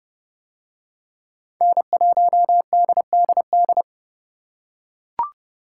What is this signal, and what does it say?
Morse code, heard as a keyed tone, about 30 wpm, 700 Hz. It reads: N1DDD